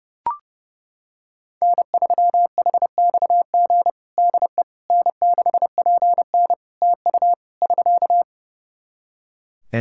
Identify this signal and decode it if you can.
Morse code, heard as a keyed tone, about 30 wpm, 700 Hz. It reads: N3HXG DE N6PN TU <SK>
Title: K1USN SST QSO Practice 30wpm